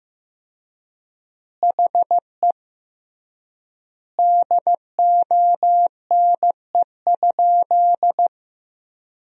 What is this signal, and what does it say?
Morse code, heard as a keyed tone, about 15 wpm, 700 Hz. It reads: HE DONE?
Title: 3x Word-Spacing - Sentences from Top 500 Words - 15wpm